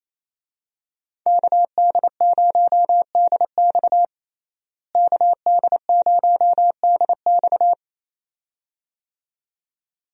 Morse code, heard as a keyed tone, about 28 wpm, 700 Hz. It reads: KD0DX KD0DX
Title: USA Call Signs - 5 Characters - Repeated 2x - 28wpm